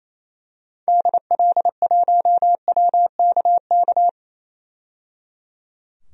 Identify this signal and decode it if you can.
Morse code, heard as a keyed tone, about 28 wpm, 700 Hz. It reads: DL1WKK
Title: DXCC-100 Call Signs - 6 Characters - 28wpm